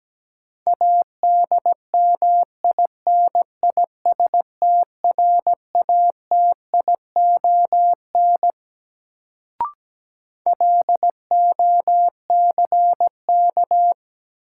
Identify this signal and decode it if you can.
Morse code, heard as a keyed tone, about 17 wpm, 700 Hz. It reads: ADMINISTRATION LOCK
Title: Sentences from Top 1300 Words - New Word Drill Alt - 17wpm